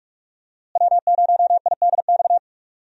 Morse code, heard as a keyed tone, about 46 wpm, 700 Hz. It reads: W0IDX
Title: USA Call Signs - 5 Characters - Repeated 3x - 45wpm